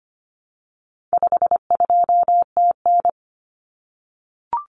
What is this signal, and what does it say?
Morse code, heard as a keyed tone, about 25 wpm, 700 Hz. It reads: 52TN
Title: Common Cut Numbers - 4-Digit 25wpm